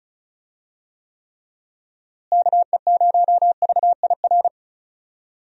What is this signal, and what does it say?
Morse code, heard as a keyed tone, about 34 wpm, 700 Hz. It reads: KE0VIR
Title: USA Call Signs - 6 Characters - Repeated 3x - 35wpm